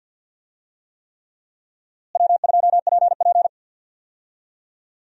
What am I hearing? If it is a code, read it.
Morse code, heard as a keyed tone, about 50 wpm, 700 Hz. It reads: W2PP
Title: USA Call Signs - 4 Characters - Repeated 3x - 50wpm